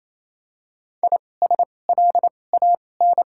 Morse code, heard as a keyed tone, about 28 wpm, spaced as for 20 wpm, 700 Hz. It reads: ISLAN
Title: 2x Character Spacing - Top 500 Words - 28wpm